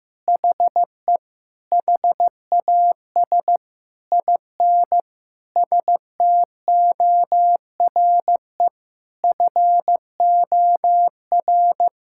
Morse code, heard as a keyed tone, about 15 wpm, 700 Hz. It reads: HE HAS IN STORE FOR
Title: Sentences from Top 1000 Words - Review Part 5 - 15wpm